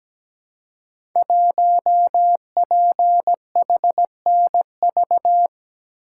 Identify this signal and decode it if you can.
Morse code, heard as a keyed tone, about 17 wpm, 700 Hz. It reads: 1PHNV
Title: Code Groups - 5 Letter:Numbers - 17wpm